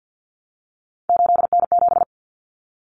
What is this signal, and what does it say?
Morse code, heard as a keyed tone, about 50 wpm, 700 Hz. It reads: 8N7
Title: Common Cut Numbers - 3-Digit 50wpm